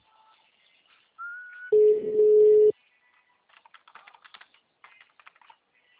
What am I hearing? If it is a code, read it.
Ringback tone (Europe, 425 Hz).
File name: out- -2012- - - .2